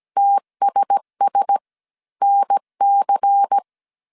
Morse code, heard as a keyed tone, about 17 wpm, spaced as for 14 wpm, 800 Hz. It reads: TSSNC
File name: CW / DET / 1 6 1500C S C 16wpm 2733